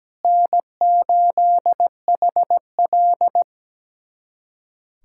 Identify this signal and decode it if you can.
Morse code, heard as a keyed tone, about 17 wpm, 700 Hz. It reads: N8HL
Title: USA Callsign DE Callsign - 4 Characters - 17wpm